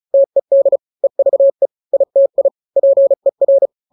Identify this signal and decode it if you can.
Morse code, heard as a keyed tone, about 34 wpm, 550 Hz. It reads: TED EVE ITI PER
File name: 3-grams 35 30 WPM